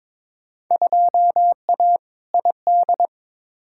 Morse code, heard as a keyed tone, about 22 wpm, 700 Hz. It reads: 2A ID
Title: Field Day Exchanges - 22wpm